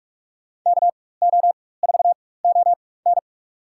Morse code, heard as a keyed tone, about 46 wpm, spaced as for 21 wpm, 700 Hz. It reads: KO4ON
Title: USA Call Signs - 5 Characters - 4x Character Spacing - 45wpm